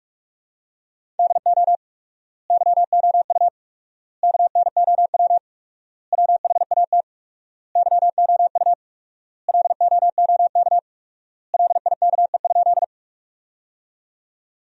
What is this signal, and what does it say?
Morse code, heard as a keyed tone, about 46 wpm, 700 Hz. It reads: DO YOU KNOW WHAT YOU LOOK LIKE?
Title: 4x Word-Spacing - Sentences from Top 100 Words - 45wpm